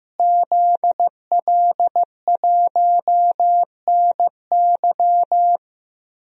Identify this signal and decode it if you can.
Morse code, heard as a keyed tone, about 15 wpm, 700 Hz. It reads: ZL1NY
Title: DXCC-100 Call Signs - 5 Characters - Repeated 3x - 15wpm